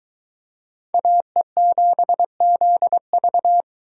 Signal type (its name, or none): Morse code